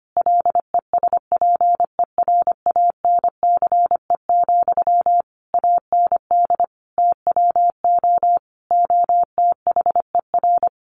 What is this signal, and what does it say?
Morse code, heard as a keyed tone, about 25 wpm, 700 Hz. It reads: LESPERANCE, AND TWO OTHER
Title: A Sound Of Thunder 25wpm